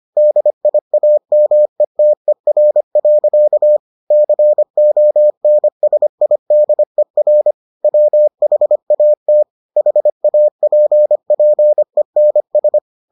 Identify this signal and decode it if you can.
Morse code, heard as a keyed tone, about 25 wpm, 600 Hz. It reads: DIAMETER. CONSIDER WHAT HAPPENS